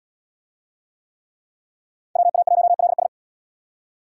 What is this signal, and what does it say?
Morse code, heard as a keyed tone, about 73 wpm, 700 Hz. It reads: WA1LS